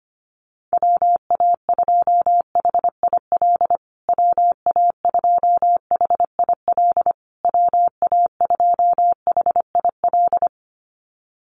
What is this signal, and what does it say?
Morse code, heard as a keyed tone, about 25 wpm, 700 Hz. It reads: WA2HIL WA2HIL WA2HIL
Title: USA Call Signs Repeated 3x 25wpm